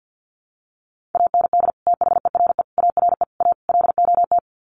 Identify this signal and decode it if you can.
Morse code, heard as a keyed tone, about 49 wpm, 700 Hz. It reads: AND THERE ARE A LOT
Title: Sets of 5 Words Encore2 50wpm